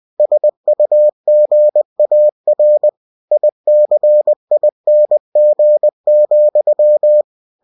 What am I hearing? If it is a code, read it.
Morse code, heard as a keyed tone, about 20 wpm, 600 Hz. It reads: SUGAR ICING,
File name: cw-books / winnie-the-pooh / winnie-20wpm-page-0103